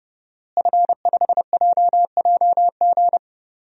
Morse code, heard as a keyed tone, about 30 wpm, 700 Hz. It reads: F5JJG
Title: DXCC-100 Call Signs - 5 Characters - Repeated 2x - 30wpm